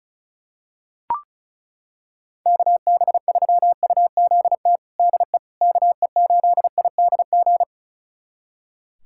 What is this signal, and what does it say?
Morse code, heard as a keyed tone, about 36 wpm, 700 Hz. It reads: KB3UZT DE KE8IDG
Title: USA Callsign DE Callsign 35wpm